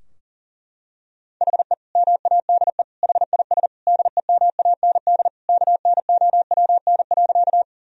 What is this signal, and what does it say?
Morse code, heard as a keyed tone, about 40 wpm, 700 Hz. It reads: HE MADE HIS DEMAND KNOWN.
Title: Sentences from Top 1000 Words - Review Part 4 - 40wpm